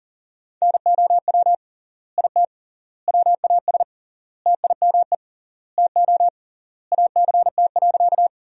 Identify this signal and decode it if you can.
Morse code, heard as a keyed tone, about 40 wpm, 700 Hz. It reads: NOW IT WAS TIME TO ACT.
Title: 3x Word-Spacing - Sentences from Top 200 Words - 40wpm